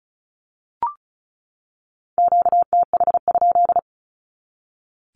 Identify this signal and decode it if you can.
Morse code, heard as a keyed tone, about 34 wpm, 700 Hz. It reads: QTH?
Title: Q-Codes 35wpm